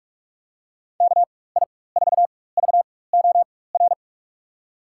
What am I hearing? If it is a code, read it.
Morse code, heard as a keyed tone, about 42 wpm, spaced as for 20 wpm, 700 Hz. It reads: KI4VOR
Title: USA Call Signs - 6 Characters - 4x Character Spacing - 45wpm